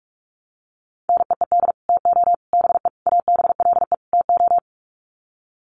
Morse code, heard as a keyed tone, about 44 wpm, 700 Hz. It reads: NEED TO BE ABLE TO
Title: Sets of 5 Words Encore 45wpm